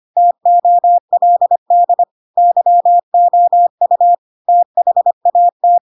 Morse code, heard as a keyed tone, about 25 wpm, 700 Hz. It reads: TOLD YOU THAT
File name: Pinocchio / Pinocchio ch7 25 25 ik1hge